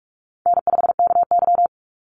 Morse code, heard as a keyed tone, about 46 wpm, 700 Hz. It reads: N5KY
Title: Easy Call Signs - 45wpm